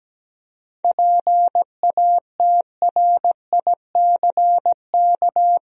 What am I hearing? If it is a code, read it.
Morse code, heard as a keyed tone, about 17 wpm, 700 Hz. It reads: PATRICK
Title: USA Common Names - 17wpm